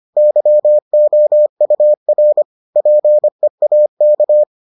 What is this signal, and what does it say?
Morse code, heard as a keyed tone, about 25 wpm, 600 Hz. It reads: YOUR PEAK